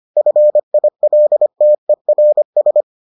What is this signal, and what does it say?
Morse code, heard as a keyed tone, about 25 wpm, 600 Hz. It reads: FILTERS